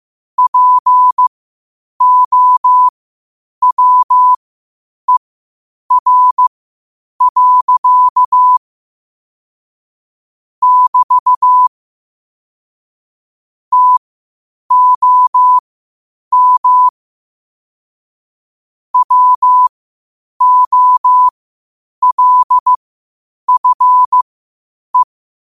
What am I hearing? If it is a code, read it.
Morse code, heard as a keyed tone, about 15 wpm, spaced as for 8 wpm, 1000 Hz. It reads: POWER. <BT> TOM WOLFE
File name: qotdfiles / 8wpm